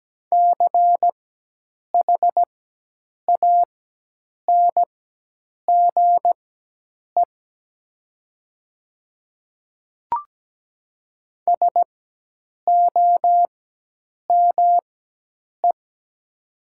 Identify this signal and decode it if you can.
Morse code, heard as a keyed tone, about 17 wpm, spaced as for 8 wpm, 700 Hz. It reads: CHANGE SOME
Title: 4x Character Spacing - Top 200 Words - 17wpm